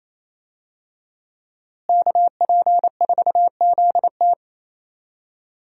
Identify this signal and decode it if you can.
Morse code, heard as a keyed tone, about 28 wpm, 700 Hz. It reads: KP4ZT
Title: USA Call Signs - 5 Characters - Repeated 2x - 28wpm